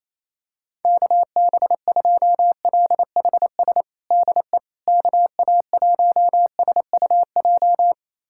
Morse code, heard as a keyed tone, about 28 wpm, 700 Hz. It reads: KB2LHS DE KA1SUJ
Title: USA Callsign DE Callsign 28wpm